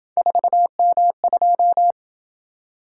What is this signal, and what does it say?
Morse code, heard as a keyed tone, about 27 wpm, 700 Hz. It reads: 4M2